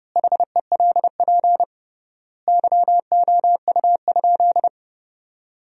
Morse code, heard as a keyed tone, about 30 wpm, 700 Hz. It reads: HELP YOU?
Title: 3x Word-Spacing - Sentences from Top 200 Words - 30wpm